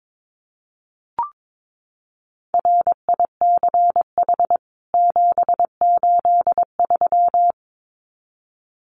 Morse code, heard as a keyed tone, about 22 wpm, 700 Hz. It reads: RICH 783
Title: CWT Exchange 22wpm